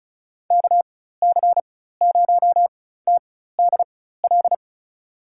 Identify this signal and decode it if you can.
Morse code, heard as a keyed tone, about 34 wpm, spaced as for 16 wpm, 700 Hz. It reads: KC0TDL
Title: USA Call Signs - 6 Characters - 4x Character Spacing - 35wpm